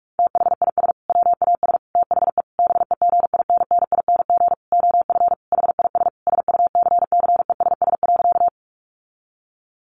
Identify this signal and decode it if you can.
Morse code, heard as a keyed tone, about 44 wpm, 700 Hz. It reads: THIS WAS THE BEGINNING OF HIS SUCCESS.
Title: Sentences from Top 900 Words 45wpm